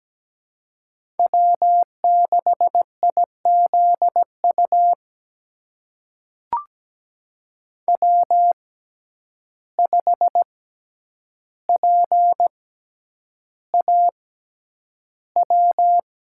Morse code, heard as a keyed tone, about 17 wpm, spaced as for 9 wpm, 700 Hz. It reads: W6IZU W5PAW
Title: USA Call Signs - 5 Characters - 6x Character Spacing - 17wpm